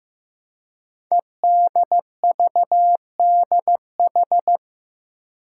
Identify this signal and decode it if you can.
Morse code, heard as a keyed tone, about 15 wpm, 700 Hz. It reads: EDVDH
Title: Code Groups - 5 Letters - 15wpm